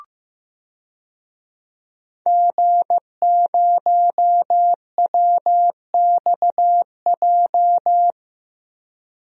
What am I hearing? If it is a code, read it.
Morse code, heard as a keyed tone, about 15 wpm, 700 Hz. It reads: G0WXJ
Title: DXCC-100 Call Signs - 5 Characters - Repeated 2x - 15wpm